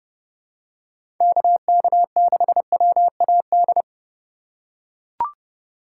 Morse code, heard as a keyed tone, about 30 wpm, 700 Hz. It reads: KK6WAD
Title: USA Call Signs - 6 Characters - 2x Character Spacing - 30wpm